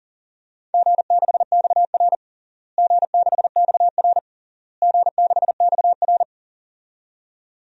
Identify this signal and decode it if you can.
Morse code, heard as a keyed tone, about 40 wpm, 700 Hz. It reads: G6XR G6XR G6XR
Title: DXCC-100 Call Signs - 4 Characters - Repeated 3x - 40wpm